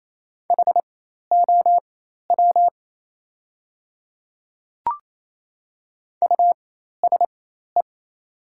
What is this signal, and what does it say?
Morse code, heard as a keyed tone, about 27 wpm, spaced as for 13 wpm, 700 Hz. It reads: HOW USE
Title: 4x Character Spacing - Top 100 Words - 28wpm